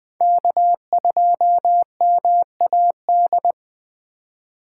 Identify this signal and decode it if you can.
Morse code, heard as a keyed tone, about 20 wpm, 700 Hz. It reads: K2MAD
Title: USA Call Signs - 3x Character Spacing - 20wpm